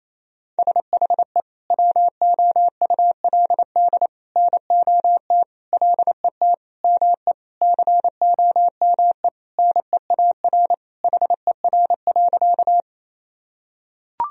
Morse code, heard as a keyed tone, about 28 wpm, 700 Hz. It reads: SHE WOULD NOT LET ME COME NEAR HER.
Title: Sentences from Top 1000 Words - Review Part 1 - 28wpm